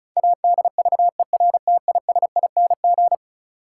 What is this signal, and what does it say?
Morse code, heard as a keyed tone, about 34 wpm, 700 Hz. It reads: ADVERTISING
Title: Sentences from Top 1000 Words - Review Part 6 - New Word Drill - 35wpm